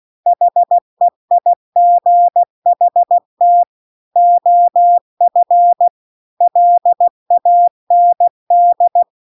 Morse code, heard as a keyed tone, about 16 wpm, 700 Hz. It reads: HEIGHT OF LAND